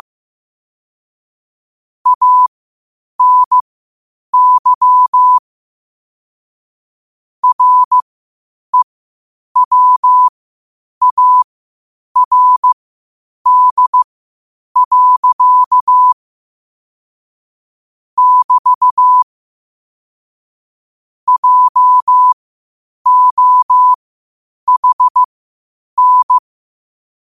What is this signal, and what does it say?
Morse code, heard as a keyed tone, about 15 wpm, spaced as for 8 wpm, 1000 Hz. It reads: ANY REWARD. <BT> JOHN
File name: qotdfiles / 8wpm